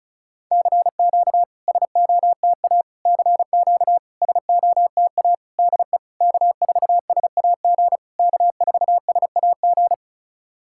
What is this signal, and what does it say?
Morse code, heard as a keyed tone, about 36 wpm, 700 Hz. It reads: CQ SOTA CQ SOTA DE K4SAG K4SAG
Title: SOTA QSO Practice 35wpm